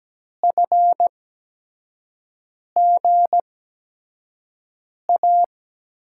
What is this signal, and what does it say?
Morse code, heard as a keyed tone, about 17 wpm, spaced as for 5 wpm, 700 Hz. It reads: FGA
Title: USA Call Signs - 8x Character Spacing - 17wpm